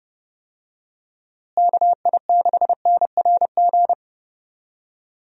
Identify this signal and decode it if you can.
Morse code, heard as a keyed tone, about 30 wpm, 700 Hz. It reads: KI6NRG
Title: USA Call Signs - 6 Characters - Repeated 3x - 30wpm